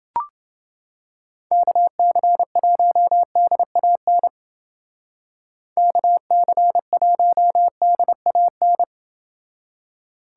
Morse code, heard as a keyed tone, about 28 wpm, 700 Hz. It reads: KC1DAN KC1DAN